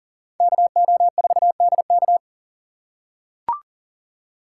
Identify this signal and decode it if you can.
Morse code, heard as a keyed tone, about 40 wpm, 700 Hz. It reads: KO4DK
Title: USA Call Signs - 3x Character Spacing - 40wpm